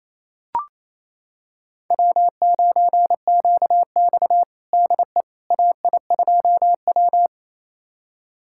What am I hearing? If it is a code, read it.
Morse code, heard as a keyed tone, about 28 wpm, 700 Hz. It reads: W9QX DE AI2W
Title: USA Callsign DE Callsign - 4 Characters - 28wpm